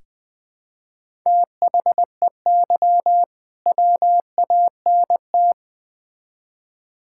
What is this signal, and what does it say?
Morse code, heard as a keyed tone, about 21 wpm, 700 Hz. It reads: THEY WANT
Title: Build A Phrase - 3-Word Phrases - 20wpm